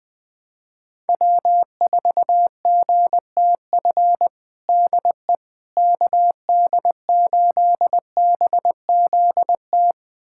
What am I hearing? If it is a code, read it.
Morse code, heard as a keyed tone, about 21 wpm, 700 Hz. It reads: W4GTF DE KD8BZT
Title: USA Callsign DE Callsign 20wpm